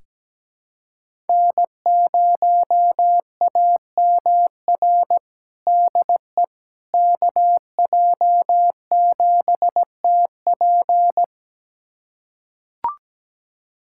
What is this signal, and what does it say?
Morse code, heard as a keyed tone, about 17 wpm, 700 Hz. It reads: N0AMR DE KJ7TP
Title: USA Callsign DE Callsign - 5 Characters - 17wpm